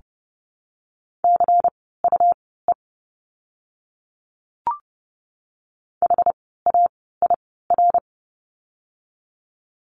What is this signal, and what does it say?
Morse code, heard as a keyed tone, about 30 wpm, spaced as for 17 wpm, 700 Hz. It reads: CUE HAIR
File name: Lesson 17f - Letter u - Words 30wpm